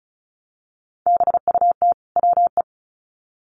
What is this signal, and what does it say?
Morse code, heard as a keyed tone, about 34 wpm, 700 Hz. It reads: BUT WE
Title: Sets of 2 Words 35wpm